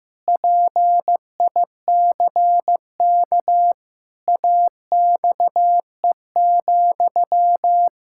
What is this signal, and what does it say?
Morse code, heard as a keyed tone, about 15 wpm, 700 Hz. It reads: PICK AXE,
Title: Pangram Challenge v2 - 15wpm